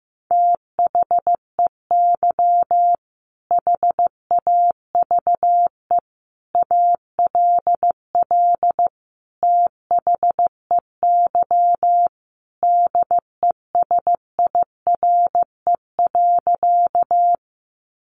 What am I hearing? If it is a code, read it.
Morse code, heard as a keyed tone, about 15 wpm, 700 Hz. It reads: THEY HAVE ALL THEY DESIRE.
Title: Sentences from Top 1100 Words 15wpm